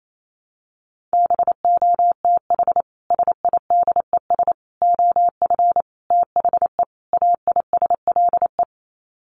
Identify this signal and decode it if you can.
Morse code, heard as a keyed tone, about 28 wpm, 700 Hz. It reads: BOTH SIDES OF THE AISLE
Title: Sets of 5 Words Encore2 28wpm